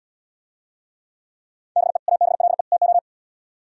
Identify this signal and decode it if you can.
Morse code, heard as a keyed tone, about 76 wpm, 700 Hz. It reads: BE ABLE TO